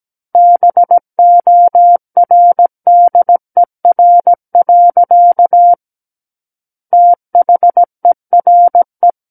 Morse code, heard as a keyed tone, about 17 wpm, 700 Hz. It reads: BORDER. THERE
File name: Book One / 08-Chapter 8 - 17 WPM